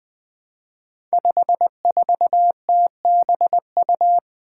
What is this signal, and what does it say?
Morse code, heard as a keyed tone, about 20 wpm, 700 Hz. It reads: 54TBU
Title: Five Challenging Letter-Number Combinations 20wpm